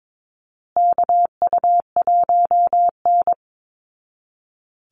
Morse code, heard as a keyed tone, about 22 wpm, 700 Hz. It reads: KU1N
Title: Call Signs Repeated 3x - 22wpm